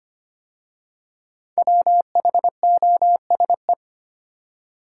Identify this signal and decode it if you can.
Morse code, heard as a keyed tone, about 25 wpm, 700 Hz. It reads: WHOSE
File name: 5 Letter Words - First 3 Letters Given - 25wpm